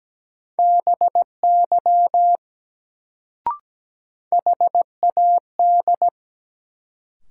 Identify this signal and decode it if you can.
Morse code, heard as a keyed tone, about 17 wpm, 700 Hz. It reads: BY HAD
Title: Sentences from Top 100 Words - New Word Drill - 17wpm